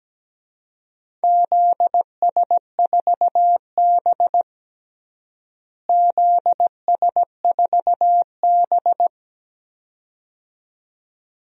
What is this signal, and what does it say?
Morse code, heard as a keyed tone, about 17 wpm, 700 Hz. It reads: ZS4B ZS4B
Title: DXCC-100 Call Signs - 4 Characters - Repeated 2x - 17wpm